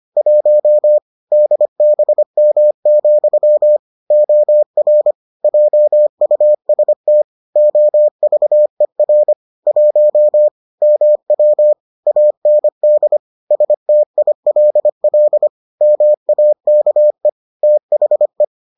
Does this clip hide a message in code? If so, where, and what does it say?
Yes: Morse code from 0.2 to 18.4 s, heard as a keyed tone, about 25 wpm, 600 Hz. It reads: 1 DBM, OR JUST OVER 1 MW AND STILL MAKE THE